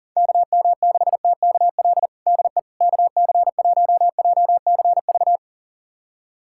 Morse code, heard as a keyed tone, about 40 wpm, 700 Hz. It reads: KM6TKL DE KC1JCV
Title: USA Callsign DE Callsign 40wpm